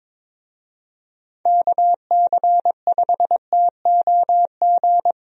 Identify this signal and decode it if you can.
Morse code, heard as a keyed tone, about 23 wpm, 700 Hz. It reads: KC5TOG
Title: USA Call Signs - 6 Characters - Repeated 2x - 22wpm